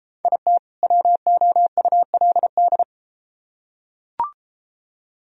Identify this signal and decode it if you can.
Morse code, heard as a keyed tone, about 33 wpm, 700 Hz. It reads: IT WOULD